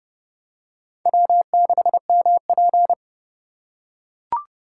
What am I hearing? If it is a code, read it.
Morse code, heard as a keyed tone, about 29 wpm, 700 Hz. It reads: W6MP